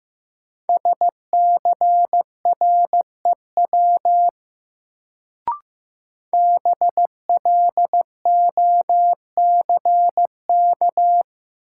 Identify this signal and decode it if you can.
Morse code, heard as a keyed tone, about 15 wpm, 700 Hz. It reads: SCREW BLOCK
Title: Sentences from Top 1600 Words - New Word Drill Alt - 15wpm